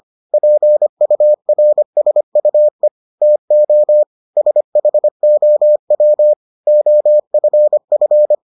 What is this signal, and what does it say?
Morse code, heard as a keyed tone, about 25 wpm, 600 Hz. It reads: PURSUE TO SHOW OFF